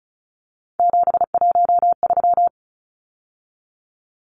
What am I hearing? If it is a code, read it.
Morse code, heard as a keyed tone, about 34 wpm, 700 Hz. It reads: 713
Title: Numbers - 3-Digit 35wpm